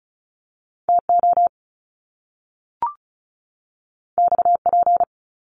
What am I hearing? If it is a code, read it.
Morse code, heard as a keyed tone, about 36 wpm, 700 Hz. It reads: TO XP
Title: Top Two Letter Combinations in Words 35wpm